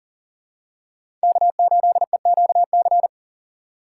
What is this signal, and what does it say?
Morse code, heard as a keyed tone, about 37 wpm, 700 Hz. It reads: K8EQC
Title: USA Call Signs - 3x Character Spacing - 40wpm